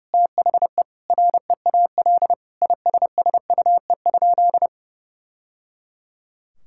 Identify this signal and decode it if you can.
Morse code, heard as a keyed tone, about 30 wpm, 700 Hz. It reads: THE REAL ISSUE?
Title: Sentences from Top 1000 Words - Review Part 6 - 30wpm